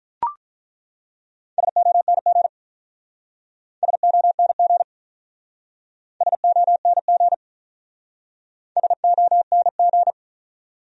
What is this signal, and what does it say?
Morse code, heard as a keyed tone, about 47 wpm, 700 Hz. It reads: SONG SONG SONG SONG